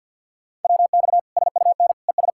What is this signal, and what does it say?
Morse code, heard as a keyed tone, about 52 wpm, 700 Hz. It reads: WX SUN ES
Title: Practice QSO - Exchange 2 - Info - 50wpm